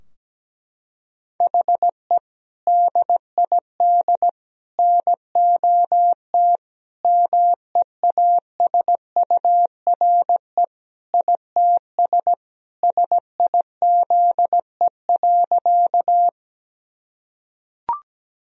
Morse code, heard as a keyed tone, about 17 wpm, 700 Hz. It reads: HE DID NOT MEASURE ITS SIZE.
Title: Sentences from Top 1000 Words - Review Part 2 - 17wpm